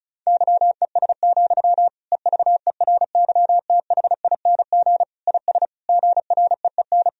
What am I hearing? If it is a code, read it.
Morse code, heard as a keyed tone, about 34 wpm, 700 Hz. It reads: YES, EVERYTHING IS GREEN
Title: Sentences from Top 1000 Words - Review Part 2 - 35wpm